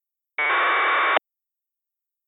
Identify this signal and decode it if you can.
Packet radio (AX.25, as in APRS).